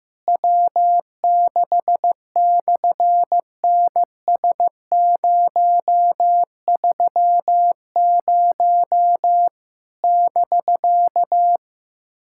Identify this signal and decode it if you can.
Morse code, heard as a keyed tone, about 15 wpm, 700 Hz. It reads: W6/NS030 <BK>
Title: SOTA QSO Practice 15wpm